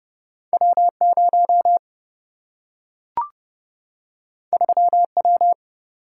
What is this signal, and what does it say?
Morse code, heard as a keyed tone, about 30 wpm, 700 Hz. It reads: W0 3W
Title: Single-Letter Single-Number 30wpm